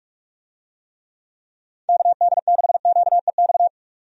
Morse code, heard as a keyed tone, about 46 wpm, 700 Hz. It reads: KD6QEX
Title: USA Call Signs - 6 Characters - 45wpm